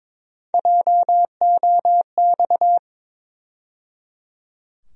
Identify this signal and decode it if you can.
Morse code, heard as a keyed tone, about 22 wpm, 700 Hz. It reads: JOX
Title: USA Call Sign Postfixes - 3 Letters 22wpm